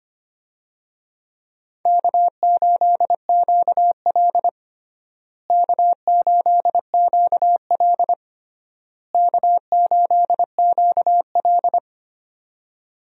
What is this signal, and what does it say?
Morse code, heard as a keyed tone, about 25 wpm, 700 Hz. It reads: K8QL K8QL K8QL
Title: USA Call Signs - 4 Characters - Repeated 3x - 25wpm